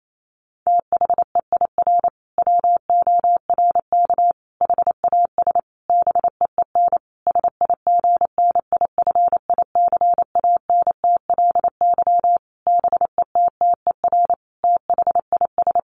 Morse code, heard as a keyed tone, about 28 wpm, 700 Hz. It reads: THEIR WORK HAS BEEN SIGNIFICANTLY BETTER THIS
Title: Sentences from Top 1500 Words 28wpm